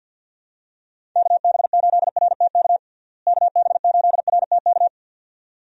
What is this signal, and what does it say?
Morse code, heard as a keyed tone, about 52 wpm, 700 Hz. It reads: KB8RTK KB8RTK
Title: USA Call Signs - 6 Characters - Repeated 2x - 50wpm